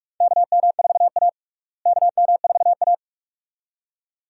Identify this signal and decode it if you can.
Morse code, heard as a keyed tone, about 47 wpm, 700 Hz. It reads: KM4A KM4A
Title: USA Call Signs - 4 Characters - Repeated 2x - 45wpm